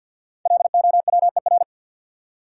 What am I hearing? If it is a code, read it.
Morse code, heard as a keyed tone, about 53 wpm, 700 Hz. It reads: LOWER
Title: Sentences from Top 700 Words - New Word Drill - 50wpm